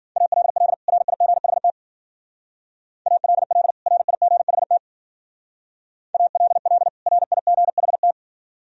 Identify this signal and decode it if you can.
Morse code, heard as a keyed tone, about 52 wpm, 700 Hz. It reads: ALL RIGHT ALL RIGHT ALL RIGHT